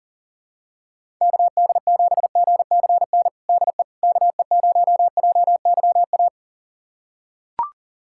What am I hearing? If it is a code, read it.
Morse code, heard as a keyed tone, about 40 wpm, 700 Hz. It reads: KD7GCN DE KE0JYA
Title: USA Callsign DE Callsign 40wpm